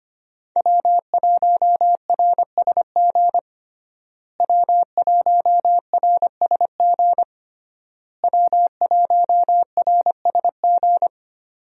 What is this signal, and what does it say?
Morse code, heard as a keyed tone, about 25 wpm, 700 Hz. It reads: W1RSG W1RSG W1RSG
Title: USA Call Signs - 5 Characters - Repeated 3x - 25wpm